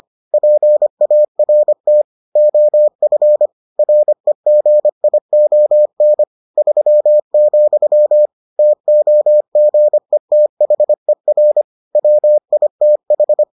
Morse code, heard as a keyed tone, about 25 wpm, 600 Hz. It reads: PART OF REGION 3, TOGETHER WITH